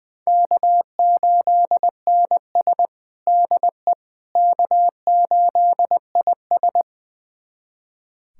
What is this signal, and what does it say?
Morse code, heard as a keyed tone, about 20 wpm, 700 Hz. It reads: K8NS DE K8IS
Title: USA Callsign DE Callsign - 4 Characters - 20wpm